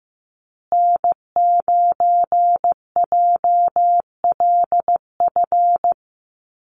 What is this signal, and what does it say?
Morse code, heard as a keyed tone, about 15 wpm, 700 Hz. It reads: N9JLF
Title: USA Call Signs Repeated 2x 15wpm